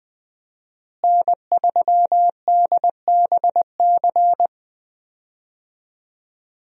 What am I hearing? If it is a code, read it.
Morse code, heard as a keyed tone, about 20 wpm, 700 Hz. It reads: N3DBC
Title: USA Call Signs - 5 Characters - Repeated 2x - 20wpm